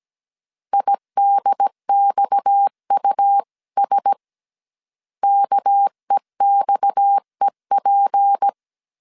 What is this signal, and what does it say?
Morse code, heard as a keyed tone, about 17 wpm, 800 Hz. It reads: IDXUS KEXEP
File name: 16wpm